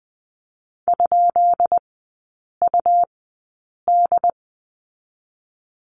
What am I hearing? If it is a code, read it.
Morse code, heard as a keyed tone, about 20 wpm, spaced as for 8 wpm, 700 Hz. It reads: ?UD
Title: 152 Lesson 25e - Letter p - Three Characters 20wpm